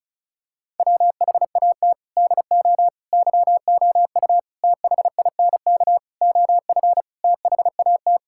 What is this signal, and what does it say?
Morse code, heard as a keyed tone, about 36 wpm, 700 Hz. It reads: WHAT DO YOU THINK OF THAT
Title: Sentences from Top 1000 Words - Review Part 6 - 35wpm